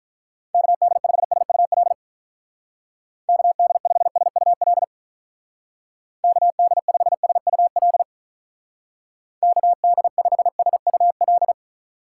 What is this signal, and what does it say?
Morse code, heard as a keyed tone, about 44 wpm, 700 Hz. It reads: KD5SUL KD5SUL KD5SUL KD5SUL
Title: Speed Racing - USA 2x3 Call Signs 35wpm